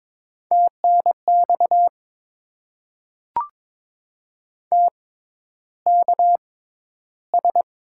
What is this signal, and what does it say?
Morse code, heard as a keyed tone, about 22 wpm, spaced as for 11 wpm, 700 Hz. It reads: TNX TKS
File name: Top 90 QSO Elements - 6x Character Spacing - 22wpm